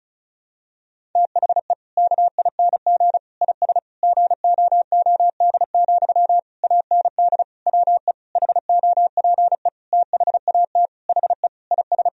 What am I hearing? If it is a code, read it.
Morse code, heard as a keyed tone, about 34 wpm, 700 Hz. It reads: THE KING IS GOOD, AND WE HOPE THAT HE IS
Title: Sentences from Top 1000 Words - Review Part 1 - 35wpm